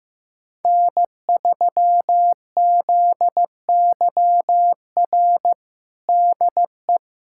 Morse code, heard as a keyed tone, about 15 wpm, 700 Hz. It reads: N3ZYR DE
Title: USA Callsign DE Callsign - 5 Characters - 15wpm